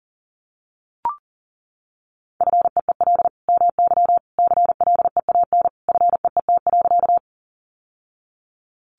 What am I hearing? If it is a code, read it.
Morse code, heard as a keyed tone, about 40 wpm, 700 Hz. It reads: FEEL MY CLEAN FEET.
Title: Sentences from Top 1100 Words 40wpm